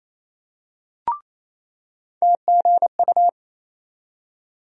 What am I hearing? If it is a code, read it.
Morse code, heard as a keyed tone, about 28 wpm, 700 Hz. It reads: TGU
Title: USA Call Sign Postfixes - 3 Letters 28wpm